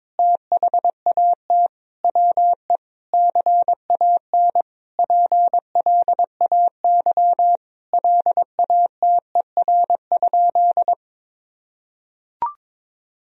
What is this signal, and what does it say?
Morse code, heard as a keyed tone, about 22 wpm, 700 Hz. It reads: THAT WE CAN PLAY LATER?
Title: Sentences from Top 1000 Words - Review Part 4 - 22wpm